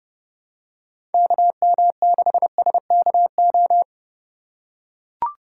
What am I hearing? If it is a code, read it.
Morse code, heard as a keyed tone, about 32 wpm, 700 Hz. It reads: KM6SKO
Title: USA Call Signs - 6 Characters - 2x Character Spacing - 30wpm